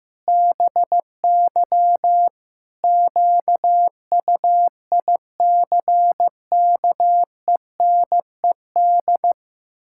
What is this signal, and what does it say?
Morse code, heard as a keyed tone, about 15 wpm, 700 Hz. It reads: BY QUICKENED
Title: Pangram Challenge v2 - 15wpm